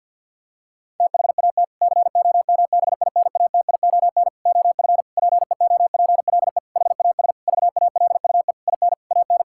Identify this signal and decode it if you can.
Morse code, heard as a keyed tone, about 50 wpm, 700 Hz. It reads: THAT COMBINATION OF PEOPLE HAS VALUE IN AN
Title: Sentences from Top 1000 Words - Review Part 6 - 50wpm